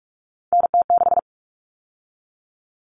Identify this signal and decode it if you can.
Morse code, heard as a keyed tone, about 44 wpm, 700 Hz. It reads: NT6
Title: All Cut Numbers - 3-Digit 45wpm